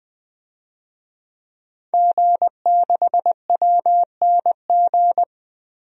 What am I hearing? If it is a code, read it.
Morse code, heard as a keyed tone, about 21 wpm, 700 Hz. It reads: G6WNG
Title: DXCC-100 Call Signs - 5 Characters - Repeated 3x - 20wpm